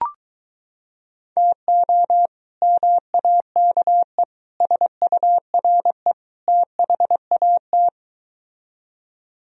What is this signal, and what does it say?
Morse code, heard as a keyed tone, about 23 wpm, 700 Hz. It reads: TO MAKE SURE THAT